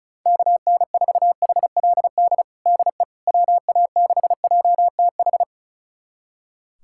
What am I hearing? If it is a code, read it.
Morse code, heard as a keyed tone, about 36 wpm, 690 Hz. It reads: KN4HLD DE WA6JTH
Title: USA Callsign DE Callsign 35wpm